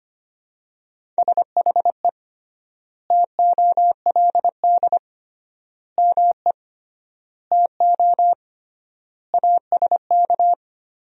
Morse code, heard as a keyed tone, about 25 wpm, 700 Hz. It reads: SHE TOLD ME TO ASK
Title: 3x Word-Spacing - Sentences from Top 400 Words - 25wpm